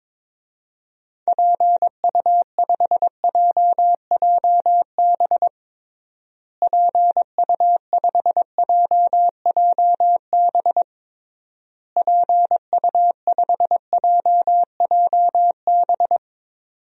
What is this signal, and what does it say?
Morse code, heard as a keyed tone, about 22 wpm, 700 Hz. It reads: PU5JJB PU5JJB PU5JJB
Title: DXCC-100 Call Signs - 6 Characters - Repeated 3x - 22wpm